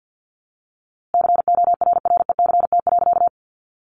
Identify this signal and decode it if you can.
Morse code, heard as a keyed tone, about 50 wpm, 700 Hz. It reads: CORRECT.
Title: A Sound Of Thunder 50wpm